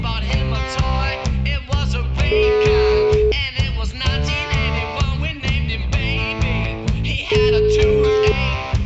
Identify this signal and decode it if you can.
Ringback tone (Europe, 425 Hz), with noise.